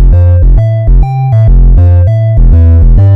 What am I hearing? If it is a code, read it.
There is no code.